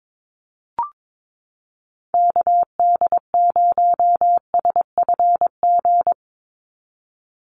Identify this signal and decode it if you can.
Morse code, heard as a keyed tone, about 22 wpm, 700 Hz. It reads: KD0SFG
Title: USA Call Signs 22wpm